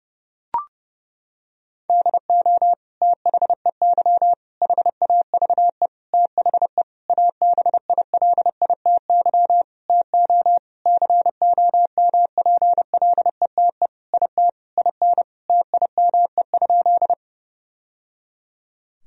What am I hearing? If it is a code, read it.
Morse code, heard as a keyed tone, about 30 wpm, 700 Hz. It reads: DO THEY HAVE THE ABILITY TO COMPLETE IT IN TIME?
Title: Sentences from Top 1000 Words - Review Part 3 - 30wpm